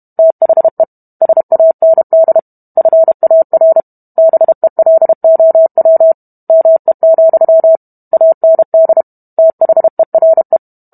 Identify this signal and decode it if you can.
Morse code, heard as a keyed tone, about 32 wpm, 650 Hz. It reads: THE SAND FAR BELOW ME, AND THERE